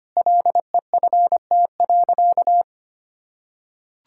Morse code, heard as a keyed tone, about 25 wpm, 700 Hz. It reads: LEFT.
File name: practice / 25wpm / Sentences with 4-Word Phrases - 25wpm